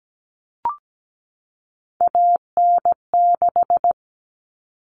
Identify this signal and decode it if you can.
Morse code, heard as a keyed tone, about 17 wpm, 700 Hz. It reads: AN6
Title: All Cut Numbers - 3-Digit 17wpm